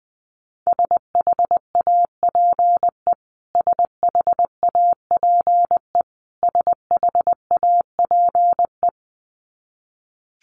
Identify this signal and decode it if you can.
Morse code, heard as a keyed tone, about 20 wpm, 700 Hz. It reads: SHAPE SHAPE SHAPE
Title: Top 500 Words Repeated 20wpm